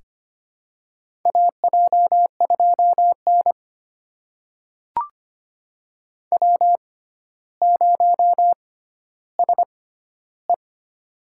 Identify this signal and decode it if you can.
Morse code, heard as a keyed tone, about 25 wpm, spaced as for 13 wpm, 700 Hz. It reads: AJ2N W0SE
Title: USA Call Signs - 4 Characters - 6x Character Spacing - 25wpm